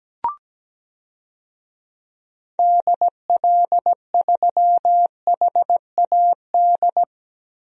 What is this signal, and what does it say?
Morse code, heard as a keyed tone, about 17 wpm, 700 Hz. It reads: DL3HAD
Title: DXCC-100 Call Signs - 6 Characters - Repeated 2x - 17wpm